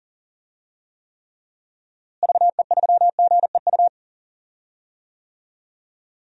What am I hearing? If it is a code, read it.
Morse code, heard as a keyed tone, about 40 wpm, 700 Hz. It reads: VE3GEU
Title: DXCC-100 Call Signs - 6 Characters - Repeated 3x - 40wpm